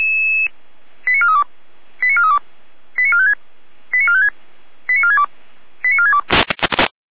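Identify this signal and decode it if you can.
Five-tone selective call (ZVEI), heard as quick runs of five tones, tones of about 70 ms. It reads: 89432 89432 89456 89456 89462 89462
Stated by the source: ZVEI five-tone selective call received on an SDR.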